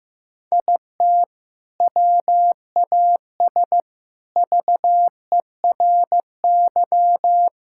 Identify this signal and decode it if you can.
Morse code, heard as a keyed tone, about 16 wpm, 700 Hz. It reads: IT WAS VERY
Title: Sentences from Top 1000 Words - Review Part 5 - 15wpm